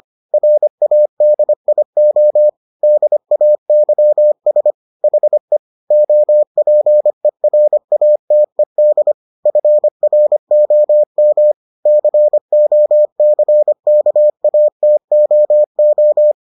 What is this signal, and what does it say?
Morse code, heard as a keyed tone, about 25 wpm, 600 Hz. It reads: RADIO DAYS HE OPERATED FROM COCKATOO